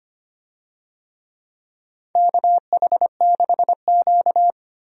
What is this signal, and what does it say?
Morse code, heard as a keyed tone, about 25 wpm, 700 Hz. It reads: KH6Q
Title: DXCC-100 Call Signs - 4 Characters - Repeated 3x - 25wpm